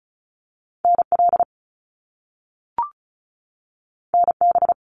Morse code, heard as a keyed tone, about 36 wpm, 700 Hz. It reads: NL NB
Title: Canadian Provinces and Territories 35wpm